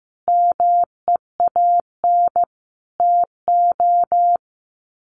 Morse code, heard as a keyed tone, about 15 wpm, 700 Hz. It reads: MEAN TO
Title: Sets of 5 Words 15wpm